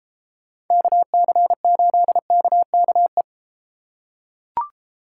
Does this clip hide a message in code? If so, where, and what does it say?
Yes: Morse code from 0.7 to 3.2 s, heard as a keyed tone, about 32 wpm, 700 Hz. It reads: KC8KKE